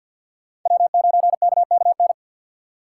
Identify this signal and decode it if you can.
Morse code, heard as a keyed tone, about 47 wpm, 700 Hz. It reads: W9KKN
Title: USA Call Signs - 5 Characters - 6x Character Spacing - 50wpm